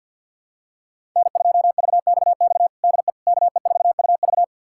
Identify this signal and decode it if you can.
Morse code, heard as a keyed tone, about 50 wpm, 700 Hz. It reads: N2VXX DE KE4UV
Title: USA Callsign DE Callsign - 5 Characters - 50wpm